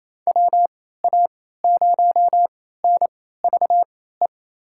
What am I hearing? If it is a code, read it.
Morse code, heard as a keyed tone, about 28 wpm, spaced as for 16 wpm, 700 Hz. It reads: WA0NVE
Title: USA Call Signs - 6 Characters - 3x Character Spacing - 28wpm